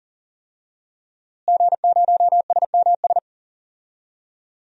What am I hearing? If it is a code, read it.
Morse code, heard as a keyed tone, about 39 wpm, 700 Hz. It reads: G0SMS
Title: DXCC-100 Call Signs - 5 Characters - Repeated 2x - 40wpm